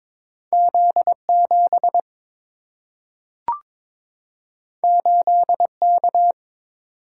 Morse code, heard as a keyed tone, about 22 wpm, 700 Hz. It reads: Z 7 8 K
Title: Single-Letter Single-Number 22wpm